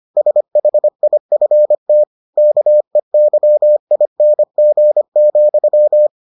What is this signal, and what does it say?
Morse code, heard as a keyed tone, about 25 wpm, 600 Hz. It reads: SHIFT KEYING,